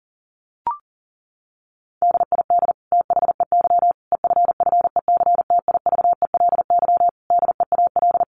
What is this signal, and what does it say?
Morse code, heard as a keyed tone, about 40 wpm, 700 Hz. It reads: DID THEY EFFECTIVELY DEAL
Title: Sentences from Top 1400 Words 40wpm